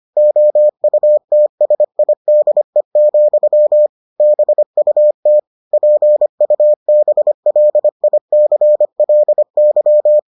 Morse code, heard as a keyed tone, about 25 wpm, 600 Hz. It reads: OUTSIDE, BUT PUBLICLY